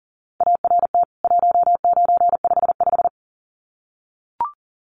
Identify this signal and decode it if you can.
Morse code, heard as a keyed tone, about 39 wpm, 700 Hz. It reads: ART 1955
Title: CWT Exchange 40wpm